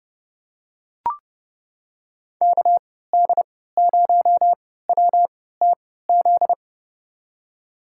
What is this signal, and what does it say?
Morse code, heard as a keyed tone, about 31 wpm, spaced as for 17 wpm, 700 Hz. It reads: KD0WTZ
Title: USA Call Signs - 3x Character Spacing - 30wpm